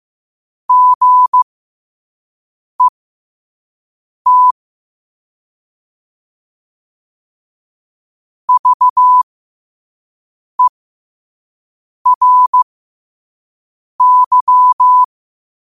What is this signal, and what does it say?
Morse code, heard as a keyed tone, about 15 wpm, spaced as for 5 wpm, 1000 Hz. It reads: GET VERY
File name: qotdfiles / 5wpm